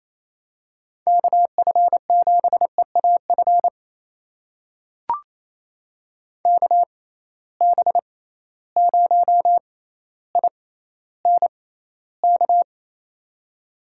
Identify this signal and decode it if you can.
Morse code, heard as a keyed tone, about 28 wpm, spaced as for 14 wpm, 700 Hz. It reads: KF7EAF KB0INK
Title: USA Call Signs - 6 Characters - 6x Character Spacing - 28wpm